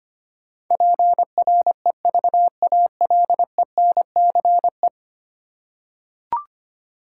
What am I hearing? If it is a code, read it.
Morse code, heard as a keyed tone, about 25 wpm, 700 Hz. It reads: PREVALENCE
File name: Ten Letter Words - 25wpm